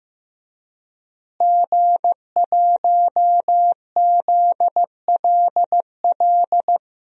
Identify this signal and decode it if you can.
Morse code, heard as a keyed tone, about 15 wpm, 700 Hz. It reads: G1ZLL
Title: DXCC-100 Call Signs - 5 Characters - Repeated 3x - 15wpm